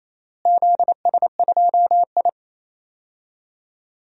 Morse code, heard as a keyed tone, about 27 wpm, 700 Hz. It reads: ZS2I
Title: DXCC-100 Call Signs - 4 Characters - Repeated 3x - 28wpm